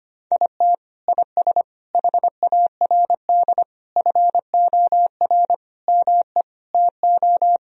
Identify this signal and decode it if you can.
Morse code, heard as a keyed tone, about 25 wpm, 700 Hz. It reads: IT IS HARD FOR ME TO
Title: Sentences from Top 1000 Words - Review Part 3 - 25wpm